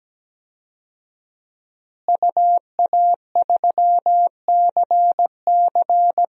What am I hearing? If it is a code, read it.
Morse code, heard as a keyed tone, about 17 wpm, 700 Hz. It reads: UA3CC
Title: DXCC-100 Call Signs - 5 Characters - Repeated 2x - 17wpm